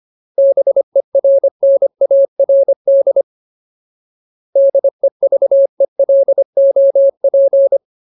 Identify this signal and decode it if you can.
Morse code, heard as a keyed tone, about 25 wpm, 550 Hz. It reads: BERNARD DEVELOP
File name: Ultimate Word List / 25WPM Ultimate